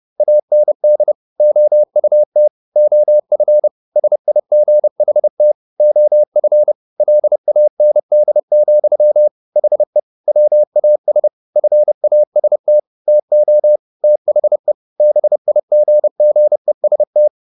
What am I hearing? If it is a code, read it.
Morse code, heard as a keyed tone, about 30 wpm, 600 Hz. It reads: AND OUT OF SIGHT OF LAND, HE WAS FAST TO THE BIGGEST
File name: the old man and the sea / the old man and the sea 1 30wpm